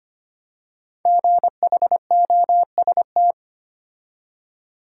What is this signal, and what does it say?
Morse code, heard as a keyed tone, about 25 wpm, 700 Hz. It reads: GHOST
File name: 5 Letter Words - First 3 Letters Given - 25wpm